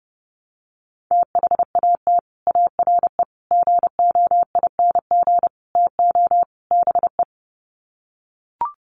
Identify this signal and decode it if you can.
Morse code, heard as a keyed tone, about 30 wpm, 700 Hz. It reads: THAT ARE GOING TO BE
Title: Sets of 5 Words Encore2 30wpm